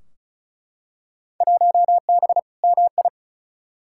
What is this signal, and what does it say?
Morse code, heard as a keyed tone, about 36 wpm, 700 Hz. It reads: 1B MI
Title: Field Day Exchanges - 35wpm